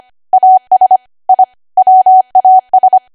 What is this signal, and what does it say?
Morse code, heard as a keyed tone, about 25 wpm, 750 Hz. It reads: AS I WAS
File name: Archive / 25 WPM / 25WPM